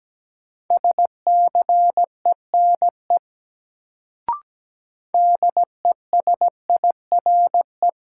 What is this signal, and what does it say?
Morse code, heard as a keyed tone, about 17 wpm, 700 Hz. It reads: SCENE DESIRE
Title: Sentences from Top 1100 Words - New Word Drill Alt - 17wpm